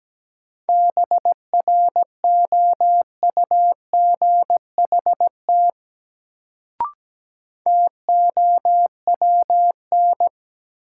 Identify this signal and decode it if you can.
Morse code, heard as a keyed tone, about 17 wpm, 700 Hz. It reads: BROUGHT TOWN
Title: Sentences from Top 500 Words - New Word Drill - 17wpm